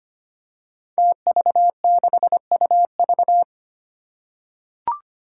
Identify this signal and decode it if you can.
Morse code, heard as a keyed tone, about 25 wpm, 700 Hz. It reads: TV6UV
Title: Five Challenging Letter-Number Combinations 25wpm